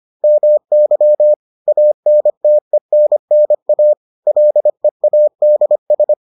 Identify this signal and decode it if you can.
Morse code, heard as a keyed tone, about 25 wpm, 600 Hz. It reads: MY ANTENNA LEADS